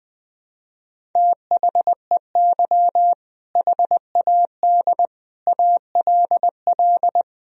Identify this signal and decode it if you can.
Morse code, heard as a keyed tone, about 20 wpm, 700 Hz. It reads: THEY HAD ALL
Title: Sentences from Top 200 Words and Abbreviations 20wpm